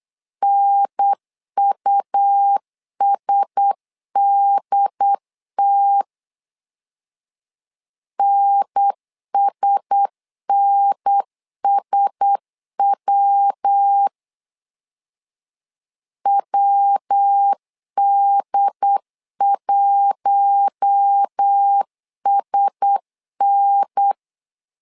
Morse code, heard as a keyed tone, about 8 wpm, 800 Hz. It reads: NUSDT NSNSW WD1SN
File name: LEZIONI / 4 Q1S 8wpm 2710